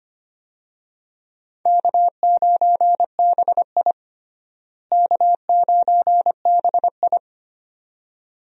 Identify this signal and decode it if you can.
Morse code, heard as a keyed tone, about 25 wpm, 700 Hz. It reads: K9BI K9BI
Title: USA Call Signs - 4 Characters - Repeated 2x - 25wpm